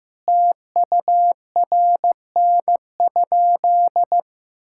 Morse code, heard as a keyed tone, about 16 wpm, 700 Hz. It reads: TURN?
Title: Sentences from Top 1600 Words 15wpm